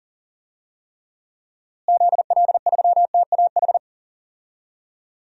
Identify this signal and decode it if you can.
Morse code, heard as a keyed tone, about 37 wpm, 700 Hz. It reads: ZL3TAH
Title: DXCC-100 Call Signs - 6 Characters - Repeated 3x - 40wpm